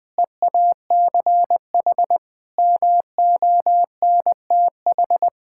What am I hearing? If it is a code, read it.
Morse code, heard as a keyed tone, about 20 wpm, 700 Hz. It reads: EACH MONTH
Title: Difficult Sentences with Hints - 20wpm